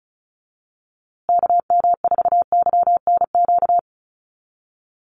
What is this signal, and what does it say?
Morse code, heard as a keyed tone, about 34 wpm, 700 Hz. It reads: KM4YNQ
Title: USA Call Signs Repeated 2x 35wpm